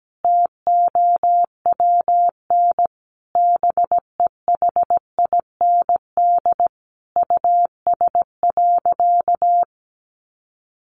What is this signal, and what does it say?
Morse code, heard as a keyed tone, about 17 wpm, 700 Hz. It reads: TOWN BEHIND US.
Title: Sentences from Top 1300 Words 17wpm